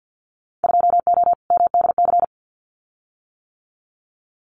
Morse code, heard as a keyed tone, about 48 wpm, 700 Hz. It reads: 2O MDC
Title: Winter Field Day Exchanges 50wpm